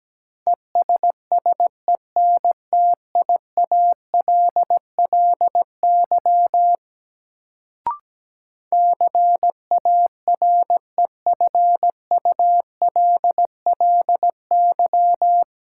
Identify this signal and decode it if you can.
Morse code, heard as a keyed tone, about 17 wpm, 700 Hz. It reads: ESSENTIALLY CAREFULLY
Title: Sentences from Top 1300 Words - New Word Drill - 17wpm